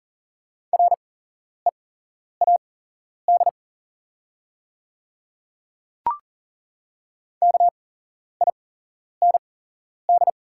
Morse code, heard as a keyed tone, about 40 wpm, spaced as for 11 wpm, 700 Hz. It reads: READ KIND
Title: 8x Character Spacing - Top 200 Words - 40wpm